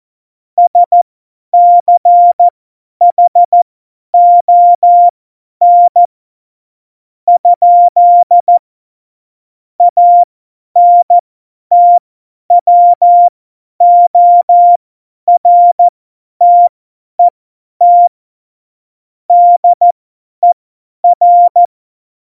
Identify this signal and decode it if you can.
Morse code, heard as a keyed tone, about 14 wpm, spaced as for 10 wpm, 700 Hz. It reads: SCHON? ANTWORTET DER
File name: Morsekurs / Witze / Witz 20 March 2018, 50 F70